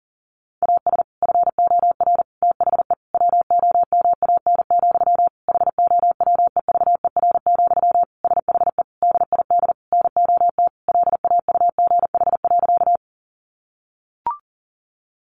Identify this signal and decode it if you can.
Morse code, heard as a keyed tone, about 40 wpm, 700 Hz. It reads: AS FOR THE WOMAN, HOWEVER, SHE DID NOT LAUGH.
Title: Sentences from Top 1100 Words 40wpm